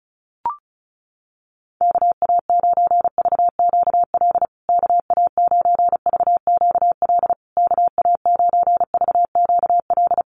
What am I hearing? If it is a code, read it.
Morse code, heard as a keyed tone, about 34 wpm, 700 Hz. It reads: KA9VQL KA9VQL KA9VQL
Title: USA Call Signs Repeated 3x 35wpm